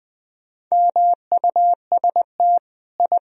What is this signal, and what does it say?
Morse code, heard as a keyed tone, about 20 wpm, 700 Hz. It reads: MUST I
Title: Sentences from Top 1000 Words - Review Part 3 - 20wpm